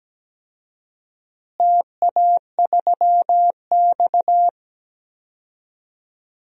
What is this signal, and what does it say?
Morse code, heard as a keyed tone, about 17 wpm, 700 Hz. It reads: TA3X
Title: DXCC-100 Call Signs - 4 Characters - Repeated 3x - 17wpm